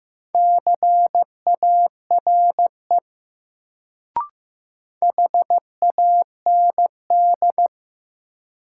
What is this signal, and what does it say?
Morse code, heard as a keyed tone, about 15 wpm, 700 Hz. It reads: CARE HAND
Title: Sets of 3 Words - New Word Drill - 15wpm